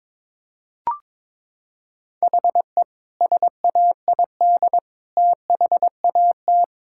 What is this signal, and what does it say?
Morse code, heard as a keyed tone, about 22 wpm, 700 Hz. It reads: HE SAID THAT
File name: practice / 22wpm / Sentences with 2-Word Phrases - 22wpm